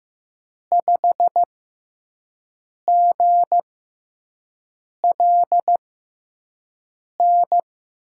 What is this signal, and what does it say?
Morse code, heard as a keyed tone, about 15 wpm, spaced as for 5 wpm, 700 Hz. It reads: 5GLN
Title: USA Call Signs - 5 Characters - 6x Character Spacing - 15wpm